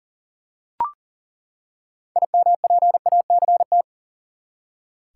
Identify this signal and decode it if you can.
Morse code, heard as a keyed tone, about 40 wpm, 700 Hz. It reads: IMPACT
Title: Six Letter Words - 40wpm